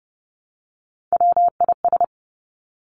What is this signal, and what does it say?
Morse code, heard as a keyed tone, about 30 wpm, 700 Hz. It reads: WIS
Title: Three Challenging Letter-Number Combinations 30wpm